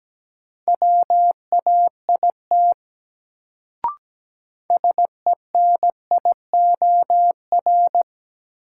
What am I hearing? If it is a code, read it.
Morse code, heard as a keyed tone, about 17 wpm, 700 Hz. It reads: WAIT SENIOR
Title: Sets of 5 Words Encore - New Word Drill - 17wpm